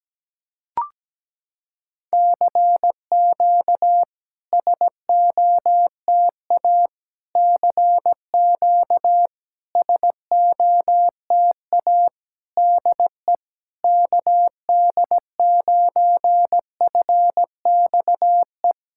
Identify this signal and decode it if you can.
Morse code, heard as a keyed tone, about 17 wpm, 700 Hz. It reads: CQ SOTA CQ SOTA DE KD9FXE
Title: SOTA QSO Practice 17wpm